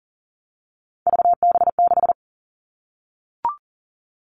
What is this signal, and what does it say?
Morse code, heard as a keyed tone, about 40 wpm, 700 Hz. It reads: VB6
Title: All Cut Numbers - 3-Digit 40wpm